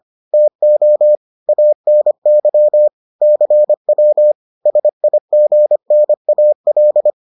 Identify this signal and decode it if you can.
Morse code, heard as a keyed tone, about 25 wpm, 600 Hz. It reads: TO ANY CW SIGNAL